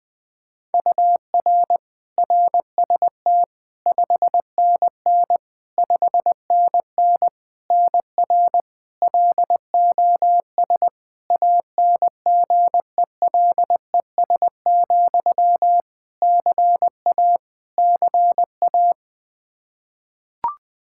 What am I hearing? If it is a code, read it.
Morse code, heard as a keyed tone, about 20 wpm, 700 Hz. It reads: UR RST 5NN 5NN NR LOS ANGELES, CA CA
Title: K1USN SST QSO Practice 20wpm